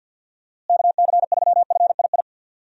Morse code, heard as a keyed tone, about 52 wpm, 700 Hz. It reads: KC3FII
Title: USA Call Signs - 6 Characters - Repeated 2x - 50wpm